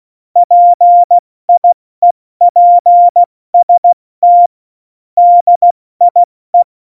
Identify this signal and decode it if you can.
Morse code, heard as a keyed tone, about 16 wpm, 700 Hz. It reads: PIEPST DIE
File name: Morsekurs / Witze / Witz 18. September 2018, 70 F80